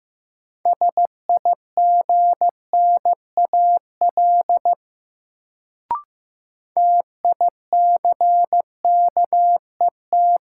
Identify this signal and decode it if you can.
Morse code, heard as a keyed tone, about 15 wpm, 700 Hz. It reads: SIGNAL TICKET
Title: Sentences from Top 1400 Words - New Word Drill Alt - 15wpm